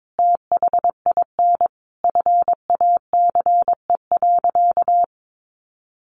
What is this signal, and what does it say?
Morse code, heard as a keyed tone, about 22 wpm, 700 Hz. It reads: THIN FACE.
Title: A Sound Of Thunder 22wpm